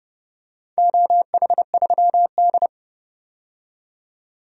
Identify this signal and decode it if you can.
Morse code, heard as a keyed tone, about 30 wpm, 700 Hz. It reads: OH3D
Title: DXCC-100 Call Signs - 4 Characters - 30wpm